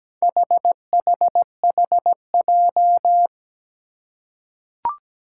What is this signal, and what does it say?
Morse code, heard as a keyed tone, about 17 wpm, 700 Hz. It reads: HHHJ
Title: Four Challenging Letter-Number Combinations 17wpm